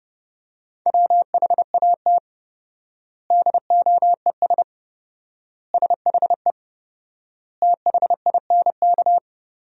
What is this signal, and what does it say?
Morse code, heard as a keyed tone, about 29 wpm, 700 Hz. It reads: WHAT DOES SHE THINK
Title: 4x Word-Spacing - Sentences from Top 200 Words - 30wpm